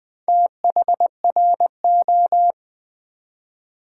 Morse code, heard as a keyed tone, about 20 wpm, 690 Hz. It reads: THRO
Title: Build A Word - 2 Letters At A Time - 6-Letter Words - 20wpm